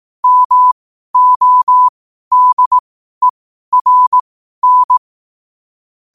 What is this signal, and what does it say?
Morse code, heard as a keyed tone, about 18 wpm, spaced as for 12 wpm, 1000 Hz. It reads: MODERN
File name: qotdfiles / 12wpm